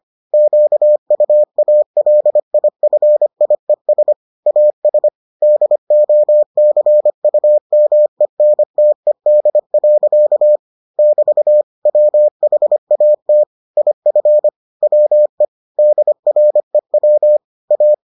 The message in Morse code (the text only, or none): QUALIFIES AS DOCUMENTED. <BT> WHAT IF WE DREW A